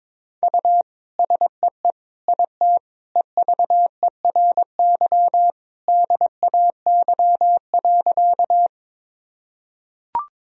Morse code, heard as a keyed tone, about 22 wpm, 700 Hz. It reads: U SEE IT EVERY DAY.
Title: Sentences from Top 200 Words and Abbreviations 22wpm